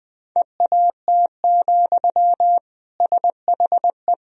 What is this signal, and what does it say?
Morse code, heard as a keyed tone, about 21 wpm, 700 Hz. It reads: EAT, SHE
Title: Sentences from Top 1000 Words - Review Part 2 - 20wpm